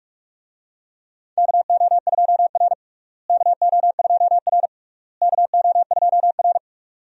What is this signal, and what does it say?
Morse code, heard as a keyed tone, about 43 wpm, 700 Hz. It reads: KO2R KO2R KO2R
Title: USA Call Signs - 4 Characters - Repeated 3x - 45wpm